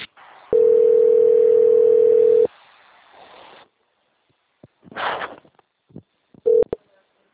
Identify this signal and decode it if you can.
Ringback tone (North America), 440 + 480 Hz.